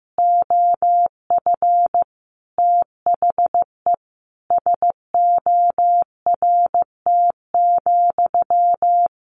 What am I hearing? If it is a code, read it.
Morse code, heard as a keyed tone, about 15 wpm, 700 Hz. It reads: OF THE SORT,
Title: Sentences from Top 1300 Words 15wpm